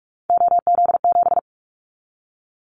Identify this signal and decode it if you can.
Morse code, heard as a keyed tone, about 46 wpm, 700 Hz. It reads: OZ7
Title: Callsign Prefixes for Top 100 DXCC - 45wpm